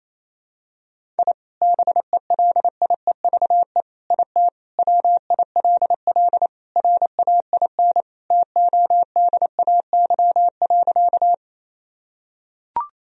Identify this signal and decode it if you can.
Morse code, heard as a keyed tone, about 28 wpm, 700 Hz. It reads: I BELIEVE IT WILL RAIN TODAY.
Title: Sentences from Top 1000 Words - Review Part 1 - 28wpm